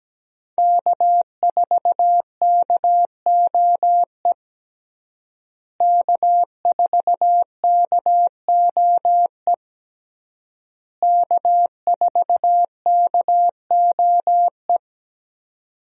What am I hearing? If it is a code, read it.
Morse code, heard as a keyed tone, about 17 wpm, 700 Hz. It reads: K4KOE K4KOE K4KOE
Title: USA Call Signs - 5 Characters - Repeated 3x - 17wpm